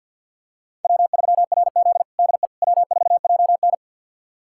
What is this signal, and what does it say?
Morse code, heard as a keyed tone, about 50 wpm, 700 Hz. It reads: W3RZ DE W4JN
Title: USA Callsign DE Callsign - 4 Characters - 50wpm